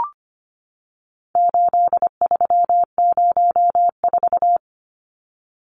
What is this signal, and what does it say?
Morse code, heard as a keyed tone, about 25 wpm, 700 Hz. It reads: 8304
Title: Numbers - 4-Digit 25wpm